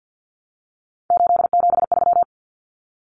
Morse code, heard as a keyed tone, about 50 wpm, 700 Hz. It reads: 873
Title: Numbers - 3-Digit 50wpm